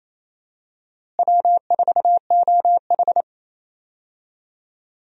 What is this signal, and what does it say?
Morse code, heard as a keyed tone, about 28 wpm, 700 Hz. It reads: W4OH
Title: USA Call Signs - 4 Characters - Repeated 3x - 28wpm